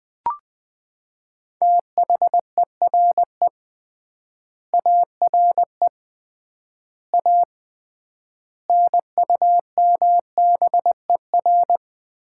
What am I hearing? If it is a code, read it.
Morse code, heard as a keyed tone, about 21 wpm, 700 Hz. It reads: THERE ARE A NUMBER
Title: 3x Word-Spacing - Sets of 5 Words - 20wpm